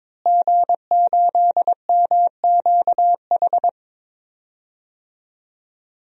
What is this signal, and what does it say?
Morse code, heard as a keyed tone, about 22 wpm, 700 Hz. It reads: G8MQH
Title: DXCC-100 Call Signs - 5 Characters - Repeated 3x - 22wpm